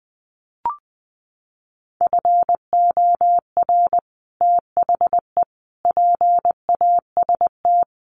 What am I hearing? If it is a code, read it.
Morse code, heard as a keyed tone, about 19 wpm, 700 Hz. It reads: FOR THE PAST
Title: Sets of 5 Words Encore2 20wpm